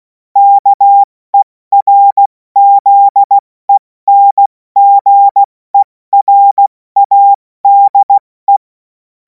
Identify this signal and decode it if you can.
Morse code, heard as a keyed tone, about 16 wpm, 800 Hz. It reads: KERZENGERADE
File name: Morsekurs / Witze / Witz 11. August 2026 70 F80